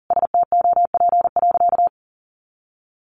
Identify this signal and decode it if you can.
Morse code, heard as a keyed tone, about 40 wpm, 700 Hz. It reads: STOP.
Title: Sentences from Top 1300 Words 40wpm